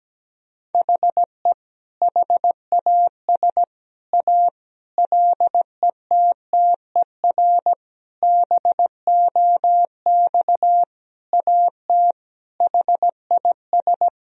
Morse code, heard as a keyed tone, about 17 wpm, 700 Hz. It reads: HE HAS A LETTER BOX AT HIS
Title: Sentences from Top 1000 Words - Review Part 2 - 17wpm